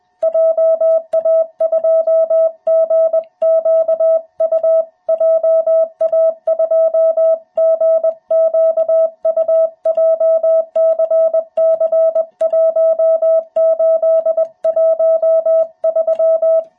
Morse code, heard as a keyed tone, about 21 wpm, 660 Hz. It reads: JA2GQU JA2GQU JCC1813
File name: jh8cbh / 2021cw